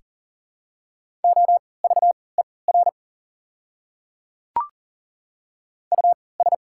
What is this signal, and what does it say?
Morse code, heard as a keyed tone, about 40 wpm, spaced as for 23 wpm, 700 Hz. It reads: OVER US
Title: 3x Character Spacing - Top 200 Words - 40wpm